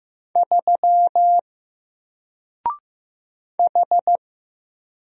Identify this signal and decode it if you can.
Morse code, heard as a keyed tone, about 15 wpm, 700 Hz. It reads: 3 H
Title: Single Character (Letter:Number:Punctuation:Prosign) 15wpm